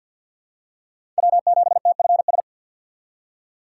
Morse code, heard as a keyed tone, about 50 wpm, 700 Hz. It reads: W7TFS
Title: USA Call Signs - 2x Character Spacing - 50wpm